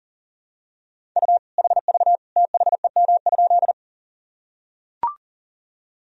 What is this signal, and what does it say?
Morse code, heard as a keyed tone, about 40 wpm, 700 Hz. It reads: U HV THEM?
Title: Sentences from Top 100 Words and Abbreviations 40wpm